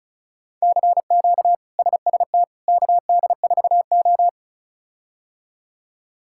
Morse code, heard as a keyed tone, about 36 wpm, 700 Hz. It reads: CQ SST KD4O
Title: K1USN SST QSO Practice - 35wpm